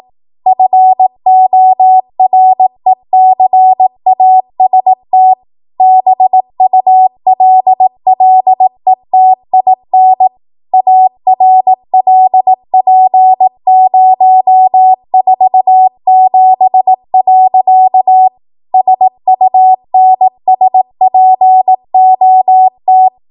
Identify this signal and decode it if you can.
Morse code, heard as a keyed tone, about 18 wpm, 750 Hz. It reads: FORECAST BULLETIN ARLP047. SUNSPOT
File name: Archive / 18 WPM / 18WPM